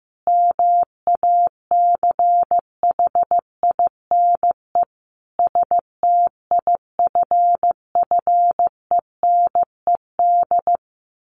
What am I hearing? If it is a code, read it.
Morse code, heard as a keyed tone, about 15 wpm, 700 Hz. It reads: MACHINE STIFFENED
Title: A Sound Of Thunder 15wpm